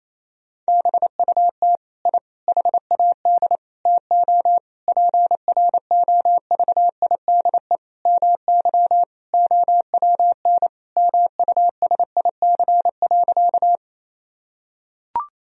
Morse code, heard as a keyed tone, about 28 wpm, 700 Hz. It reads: BUT I HAD TO PROVIDE MY OWN MUSIC.
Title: Sentences from Top 1000 Words - Review Part 3 - 28wpm